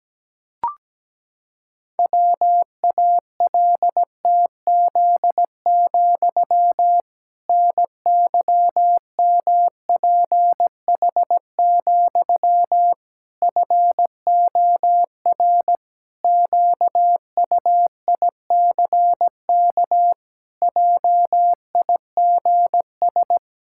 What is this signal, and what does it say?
Morse code, heard as a keyed tone, about 17 wpm, 700 Hz. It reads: WALTZ, NYMPH, FOR QUICK JIGS
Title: Pangram Challenge v2 - 17wpm